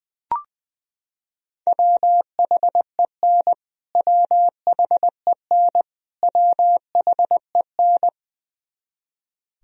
Morse code, heard as a keyed tone, about 19 wpm, 700 Hz. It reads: WHEN WHEN WHEN
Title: Top 100 Words Repeated 20wpm